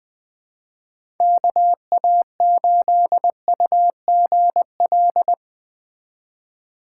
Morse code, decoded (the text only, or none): KA8UGL